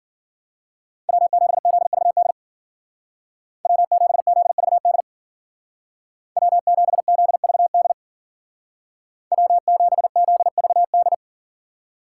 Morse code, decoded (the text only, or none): W7ZVD W7ZVD W7ZVD W7ZVD